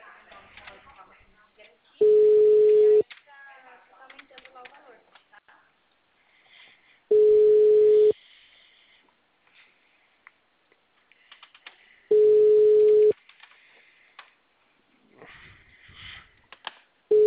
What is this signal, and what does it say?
Ringback tone (Europe, 425 Hz), with speech.
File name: out- -2007- - - .79462